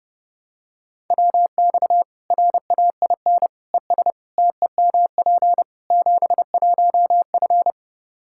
Morse code, heard as a keyed tone, about 30 wpm, 700 Hz. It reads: WX RAIN ES TEMP 71F
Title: K1USN SST QSO Practice 30wpm